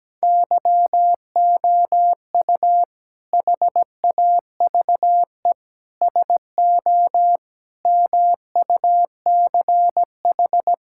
Morse code, heard as a keyed tone, about 17 wpm, 700 Hz. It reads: YOU HAVE SO MUCH
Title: Sentences from Top 1600 Words 17wpm